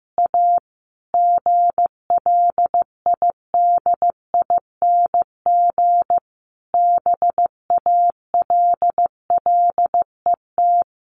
Morse code, heard as a keyed tone, about 15 wpm, 700 Hz. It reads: A GLIDING BALLET
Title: A Sound Of Thunder 15wpm